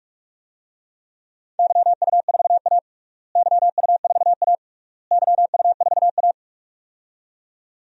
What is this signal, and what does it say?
Morse code, heard as a keyed tone, about 46 wpm, 700 Hz. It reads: YU4A YU4A YU4A
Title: DXCC-100 Call Signs - 4 Characters - Repeated 3x - 45wpm